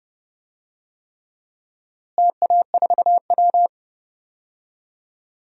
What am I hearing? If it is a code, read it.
Morse code, heard as a keyed tone, about 30 wpm, 700 Hz. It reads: TA4W
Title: DXCC-100 Call Signs - 4 Characters - Repeated 3x - 30wpm